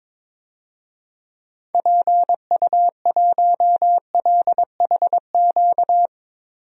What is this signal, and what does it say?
Morse code, heard as a keyed tone, about 22 wpm, 700 Hz. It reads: PU1LHQ
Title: DXCC-100 Call Signs - 6 Characters - 22wpm